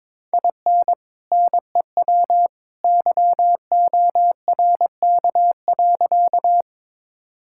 Morse code, heard as a keyed tone, about 21 wpm, 700 Hz. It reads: IN NEW YORK.
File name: practice / 22wpm / Sentences with 2-Word Phrases - 22wpm